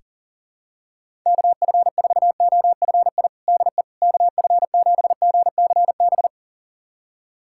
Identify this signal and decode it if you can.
Morse code, heard as a keyed tone, about 40 wpm, 700 Hz. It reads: KF4OFI DE KF7GCB
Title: USA Callsign DE Callsign - 6 Characters - 40wpm